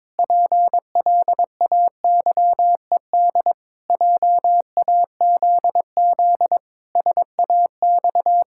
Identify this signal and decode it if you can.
Morse code, heard as a keyed tone, about 22 wpm, 700 Hz. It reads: PLAYED JAZZ SAX
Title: Pangram Challenge v2 - 22wpm